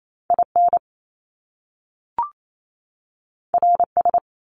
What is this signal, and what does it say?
Morse code, heard as a keyed tone, about 28 wpm, 700 Hz. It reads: IN RS
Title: Top Two Letter Combinations in Words 28wpm